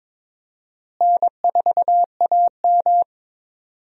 Morse code, heard as a keyed tone, about 22 wpm, 700 Hz. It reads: N4AM
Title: USA Call Signs - 4 Characters - Repeated 3x - 22wpm